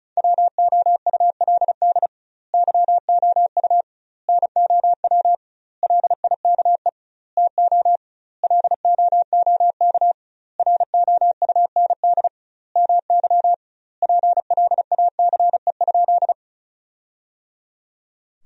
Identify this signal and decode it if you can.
Morse code, heard as a keyed tone, about 36 wpm, 700 Hz. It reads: WOULD YOU NOW LIKE TO LOOK ROUND MY PLACE?
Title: 2x Word-Spacing - Sentences from Top 200 Words - 35wpm